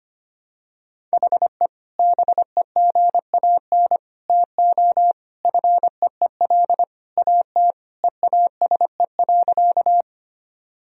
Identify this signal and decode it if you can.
Morse code, heard as a keyed tone, about 25 wpm, 700 Hz. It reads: HE BEGAN TO FEEL AT EASE.
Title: Sentences from Top 1000 Words - Review Part 1 - 25wpm